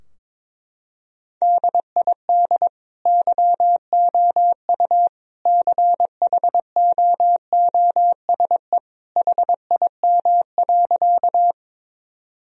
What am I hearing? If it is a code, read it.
Morse code, heard as a keyed tone, about 22 wpm, 700 Hz. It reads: DID YOU CHOOSE HIM.
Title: Sentences from Top 1000 Words - Review Part 4 - 22wpm